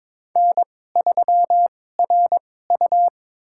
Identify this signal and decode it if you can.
Morse code, heard as a keyed tone, about 22 wpm, spaced as for 16 wpm, 700 Hz. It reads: N3RU
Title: USA Call Signs - 4 Characters - 2x Character Spacing - 22wpm